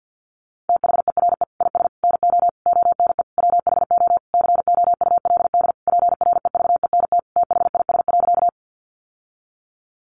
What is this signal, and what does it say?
Morse code, heard as a keyed tone, about 50 wpm, 700 Hz. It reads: THERE IS NO ONE WHO COULD PREVENT THIS.
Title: Sentences from Top 1100 Words 50wpm